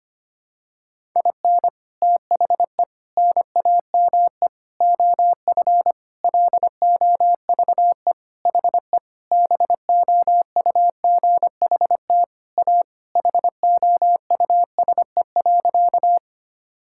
Morse code, heard as a keyed tone, about 25 wpm, 700 Hz. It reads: IN THE NAME OF LOVE HE BOUGHT A HOUSE.
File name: practice / 25wpm / Sentences with 4-Word Phrases - 25wpm